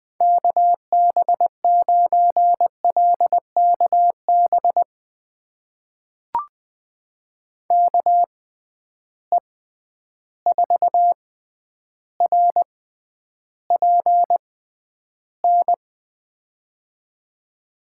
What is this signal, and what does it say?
Morse code, heard as a keyed tone, about 20 wpm, spaced as for 10 wpm, 700 Hz. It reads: KB9LKB KE4RPN
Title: USA Call Signs - 6 Characters - 6x Character Spacing - 20wpm